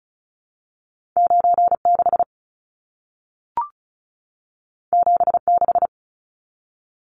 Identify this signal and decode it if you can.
Morse code, heard as a keyed tone, about 36 wpm, 700 Hz. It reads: 96 76
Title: Numbers - 2-Digit 35wpm